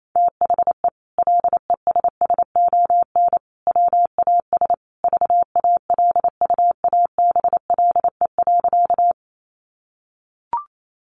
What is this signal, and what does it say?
Morse code, heard as a keyed tone, about 28 wpm, 700 Hz. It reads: THE LESSON WAS VALUABLE.
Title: Sentences from Top 1200 Words 28wpm